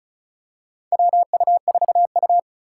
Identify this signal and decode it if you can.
Morse code, heard as a keyed tone, about 34 wpm, 700 Hz. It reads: WU4U
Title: USA Call Signs - 4 Characters - 8x Character Spacing - 35wpm